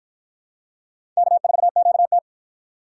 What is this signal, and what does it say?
Morse code, heard as a keyed tone, about 53 wpm, 700 Hz. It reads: K4QT